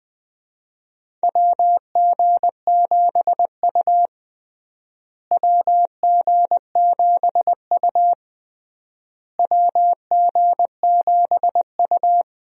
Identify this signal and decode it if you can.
Morse code, heard as a keyed tone, about 20 wpm, 700 Hz. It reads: WG7U WG7U WG7U
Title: USA Call Signs - 4 Characters - Repeated 3x - 20wpm